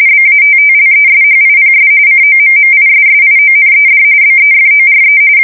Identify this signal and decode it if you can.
RTTY, about 45.45 baud. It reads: BATTLESHIPS COMMISSIONED BEFORE
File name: fsk170 / USS-New-Jersey